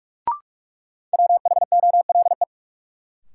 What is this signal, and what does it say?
Morse code, heard as a keyed tone, about 46 wpm, 700 Hz. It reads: WHOLE
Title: Sets of 2 Words - New Word Drill - 45wpm